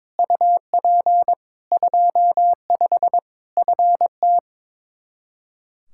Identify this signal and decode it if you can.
Morse code, heard as a keyed tone, about 22 wpm, 700 Hz. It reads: UP 25 FT
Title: Practice QSO - Exchange 2 - Info - 22wpm